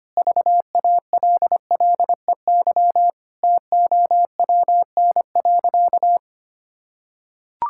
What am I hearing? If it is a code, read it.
Morse code, heard as a keyed tone, about 25 wpm, 700 Hz. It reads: VALLEY TOWN.
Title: Pangram Challenge v2 - 25wpm